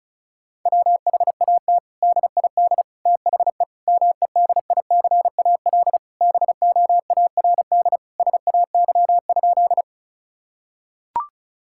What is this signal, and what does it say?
Morse code, heard as a keyed tone, about 34 wpm, 700 Hz. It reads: WHAT DID THE MEDICAL BOARD SAY?
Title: Sentences from Top 1000 Words - Review Part 5 - 35wpm